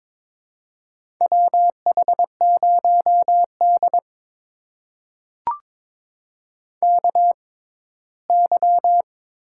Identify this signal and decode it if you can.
Morse code, heard as a keyed tone, about 22 wpm, spaced as for 15 wpm, 700 Hz. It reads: WH0D KY
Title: USA Call Signs - 4 Characters - 6x Character Spacing - 22wpm